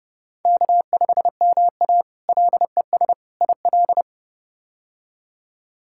Morse code, heard as a keyed tone, about 29 wpm, 700 Hz. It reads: K5MA LES IL
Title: K1USN SST QSO Practice - 30wpm